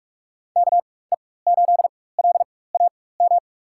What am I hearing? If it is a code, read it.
Morse code, heard as a keyed tone, about 46 wpm, spaced as for 21 wpm, 700 Hz. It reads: KE8LAM
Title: USA Call Signs - 6 Characters - 4x Character Spacing - 45wpm